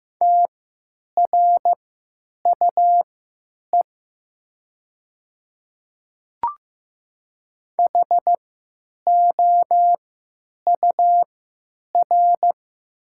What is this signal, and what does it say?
Morse code, heard as a keyed tone, about 15 wpm, spaced as for 9 wpm, 700 Hz. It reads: TRUE HOUR
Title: 3x Character Spacing - Top 500 Words - 15wpm